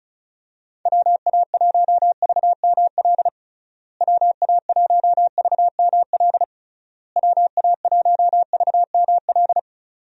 Morse code, heard as a keyed tone, about 34 wpm, 700 Hz. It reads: WA1VML WA1VML WA1VML
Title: USA Call Signs - 6 Characters - Repeated 3x - 35wpm